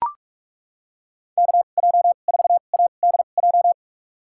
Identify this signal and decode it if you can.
Morse code, heard as a keyed tone, about 41 wpm, spaced as for 31 wpm, 700 Hz. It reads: KJ4ADJ
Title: USA Call Signs - 6 Characters - 2x Character Spacing - 45wpm